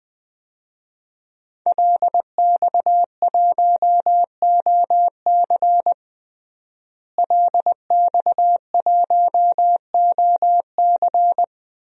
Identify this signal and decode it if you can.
Morse code, heard as a keyed tone, about 20 wpm, 700 Hz. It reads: LX1OC LX1OC
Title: DXCC-100 Call Signs - 5 Characters - Repeated 2x - 20wpm